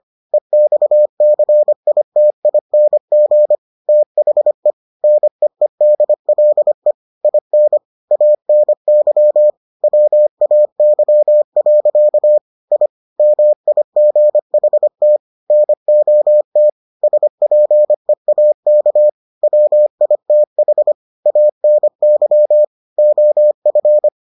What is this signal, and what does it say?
Morse code, heard as a keyed tone, about 25 wpm, 600 Hz. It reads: EXCITING THE NEEDLE IN ANY WAY. I MIGHT NOT SPEAK WITH ANY OF